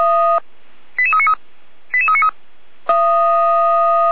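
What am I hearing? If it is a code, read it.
Five-tone selective call (ZVEI), heard as quick runs of five tones, tones of about 70 ms. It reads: 88393 88393
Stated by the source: ZVEI five-tone selective call received by SDR.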